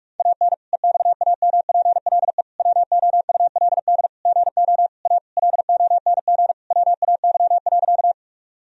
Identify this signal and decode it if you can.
Morse code, heard as a keyed tone, about 41 wpm, 700 Hz. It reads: AN EXAMPLE WOULD GO A LONG WAY.
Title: Sentences from Top 1000 Words - Review Part 1 - 45wpm